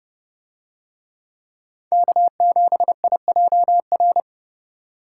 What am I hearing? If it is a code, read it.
Morse code, heard as a keyed tone, about 29 wpm, 700 Hz. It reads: K7IJR